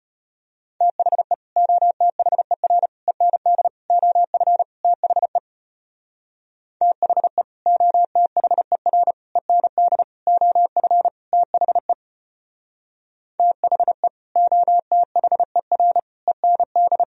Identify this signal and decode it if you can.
Morse code, heard as a keyed tone, about 33 wpm, 700 Hz. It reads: THE OTHER END OF THE THE OTHER END OF THE THE OTHER END